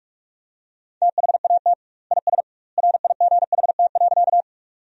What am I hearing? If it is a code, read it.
Morse code, heard as a keyed tone, about 46 wpm, 700 Hz. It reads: THAT IS RIGHT.
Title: 2x Word-Spacing - Sentences from Top 200 Words - 45wpm